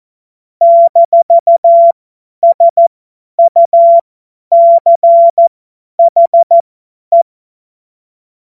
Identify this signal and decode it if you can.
Morse code, heard as a keyed tone, about 14 wpm, spaced as for 10 wpm, 680 Hz. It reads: -SUCHE
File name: Morsekurs / Witze / Witz 09. Dezember 2025 50 F70